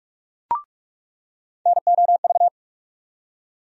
Morse code, heard as a keyed tone, about 44 wpm, 700 Hz. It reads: NOV
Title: USA Call Sign Postfixes - 3 Letters 45wpm